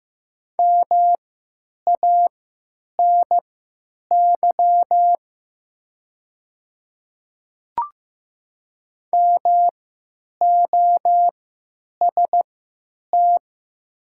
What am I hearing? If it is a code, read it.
Morse code, heard as a keyed tone, about 15 wpm, spaced as for 9 wpm, 700 Hz. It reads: MANY MOST
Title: 3x Character Spacing - Top 100 Words - 15wpm